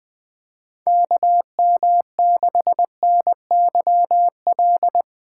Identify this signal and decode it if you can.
Morse code, heard as a keyed tone, about 21 wpm, 700 Hz. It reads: KM6NYL
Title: USA Call Signs - 6 Characters - 2x Character Spacing - 20wpm